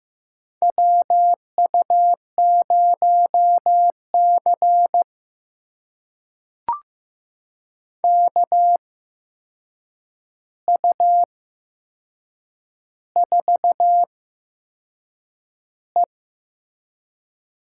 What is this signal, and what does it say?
Morse code, heard as a keyed tone, about 15 wpm, spaced as for 6 wpm, 700 Hz. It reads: WU0C KU4E
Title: USA Call Signs - 8x Character Spacing - 15wpm